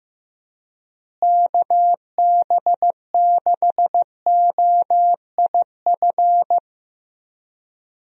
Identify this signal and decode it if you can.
Morse code, heard as a keyed tone, about 15 wpm, 700 Hz. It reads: KB6OIF
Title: USA Call Signs - 6 Characters - Repeated 2x - 15wpm